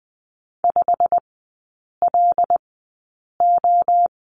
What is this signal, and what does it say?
Morse code, heard as a keyed tone, about 19 wpm, spaced as for 8 wpm, 700 Hz. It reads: 5LO
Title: 133 Lesson 22e - Question Mark - Three Characters 20wpm